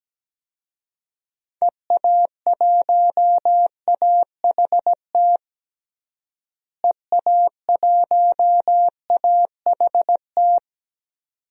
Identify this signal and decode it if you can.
Morse code, heard as a keyed tone, about 17 wpm, 700 Hz. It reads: EA1AHT EA1AHT
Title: DXCC-100 Call Signs - 6 Characters - Repeated 2x - 17wpm